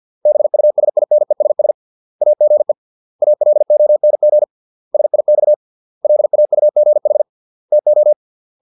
Morse code, heard as a keyed tone, about 50 wpm, 600 Hz. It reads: BUSINESS AGE ALONG SIX LAUGH TO